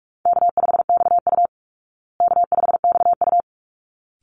Morse code, heard as a keyed tone, about 44 wpm, 700 Hz. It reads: K5XU K5XU
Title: Call Signs Repeated 3x - 45wpm